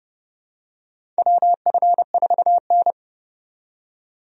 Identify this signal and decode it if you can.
Morse code, heard as a keyed tone, about 32 wpm, 700 Hz. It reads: WF4N
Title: USA Call Signs - 4 Characters - 30wpm